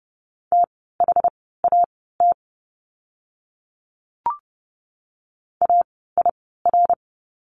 Morse code, heard as a keyed tone, about 30 wpm, spaced as for 17 wpm, 700 Hz. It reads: THAT AIR
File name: Lesson 17f - Letter u - Words 30wpm